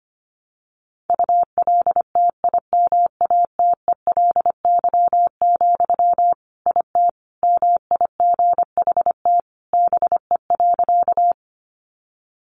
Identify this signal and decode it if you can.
Morse code, heard as a keyed tone, about 25 wpm, 700 Hz. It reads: ULTIMATELY, IT MIGHT BE.
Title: Sentences from Top 1200 Words 25wpm